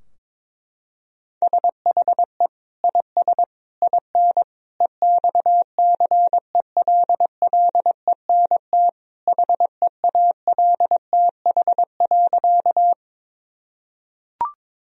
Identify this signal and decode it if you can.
Morse code, heard as a keyed tone, about 22 wpm, 700 Hz. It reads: SHE IS IN EXCELLENT HEALTH.
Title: Sentences from Top 1000 Words - Review Part 3 - 22wpm